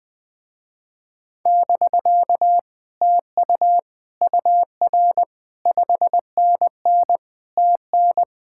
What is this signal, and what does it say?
Morse code, heard as a keyed tone, about 20 wpm, 700 Hz. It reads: <BK> TU UR 5NN TN